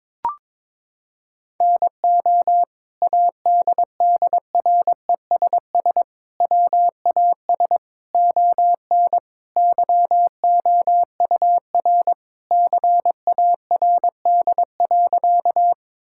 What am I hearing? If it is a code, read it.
Morse code, heard as a keyed tone, about 22 wpm, 700 Hz. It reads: NO ADDRESS WAS ON YOUR CARD.
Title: Sentences from Top 1000 Words - Review Part 6 - 22wpm